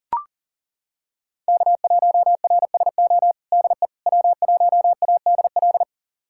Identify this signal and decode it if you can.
Morse code, heard as a keyed tone, about 40 wpm, 700 Hz. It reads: K1RSO DE W1ADL
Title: USA Callsign DE Callsign - 5 Characters - 40wpm